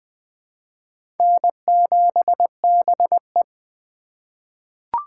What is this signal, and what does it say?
Morse code, heard as a keyed tone, about 20 wpm, 700 Hz. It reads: N7BE
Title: USA Call Signs - 6x Character Spacing - 20wpm